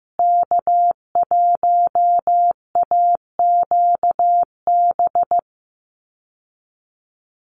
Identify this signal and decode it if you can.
Morse code, heard as a keyed tone, about 15 wpm, 700 Hz. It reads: K1AQB
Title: USA Call Signs Repeated 3x 15wpm